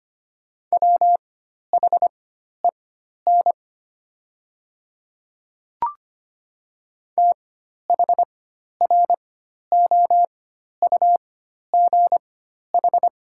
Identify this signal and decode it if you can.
Morse code, heard as a keyed tone, about 25 wpm, spaced as for 12 wpm, 700 Hz. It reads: WHEN THROUGH
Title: 4x Character Spacing - Top 200 Words - 25wpm